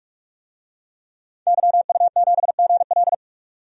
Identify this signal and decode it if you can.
Morse code, heard as a keyed tone, about 41 wpm, 700 Hz. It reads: YU7GL
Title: DXCC-100 Call Signs - 5 Characters - 45wpm